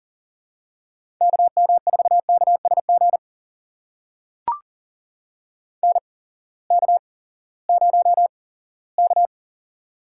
Morse code, heard as a keyed tone, about 37 wpm, spaced as for 19 wpm, 700 Hz. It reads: KM4KSG NK0K
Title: USA Call Signs - 8x Character Spacing - 40wpm